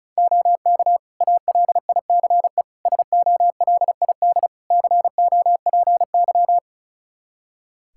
Morse code, heard as a keyed tone, about 36 wpm, 700 Hz. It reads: OK ALICE SOLID COPY
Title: Practice QSO - Exchange 3 - More Info - 35wpm